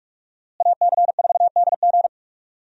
Morse code, heard as a keyed tone, about 47 wpm, 700 Hz. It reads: AC4DG
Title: USA Call Signs - 5 Characters - Repeated 3x - 45wpm